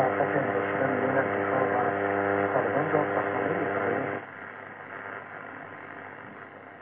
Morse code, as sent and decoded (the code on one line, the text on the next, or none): none